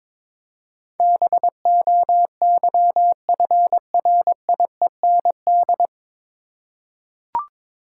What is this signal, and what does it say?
Morse code, heard as a keyed tone, about 21 wpm, 700 Hz. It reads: BOYFRIEND
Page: Morse Code Ninja practice 22wpm